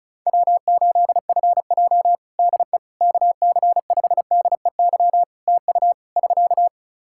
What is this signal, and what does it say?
Morse code, heard as a keyed tone, about 36 wpm, 700 Hz. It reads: W8FJ DE KC5DEY TU <SK>
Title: Practice QSO - Exchange 4 - Ending - 35wpm